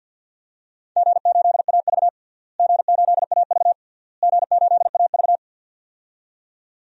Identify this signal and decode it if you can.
Morse code, heard as a keyed tone, about 52 wpm, 700 Hz. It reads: G8AV G8AV G8AV
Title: DXCC-100 Call Signs - 4 Characters - Repeated 3x - 50wpm